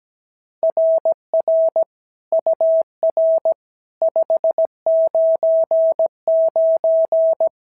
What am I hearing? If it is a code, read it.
Morse code, heard as a keyed tone, about 16 wpm, 650 Hz. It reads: RR UR 599
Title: SOTA QSO Practice 17wpm